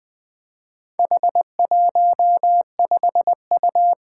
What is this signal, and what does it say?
Morse code, heard as a keyed tone, about 20 wpm, 700 Hz. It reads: H15U
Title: Four Challenging Letter-Number Combinations 20wpm